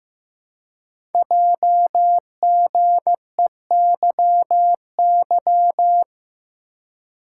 Morse code, heard as a keyed tone, about 15 wpm, 700 Hz. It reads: JGEYY
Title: Code Groups - 5 Letters - 15wpm